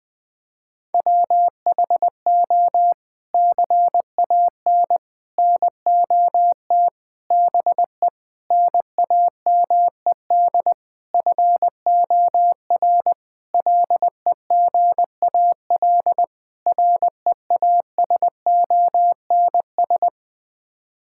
Morse code, heard as a keyed tone, about 20 wpm, 700 Hz. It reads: WHO CAN NOT BE NAMED FOR LEGAL REASONS
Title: Sets of 8 Words - 20wpm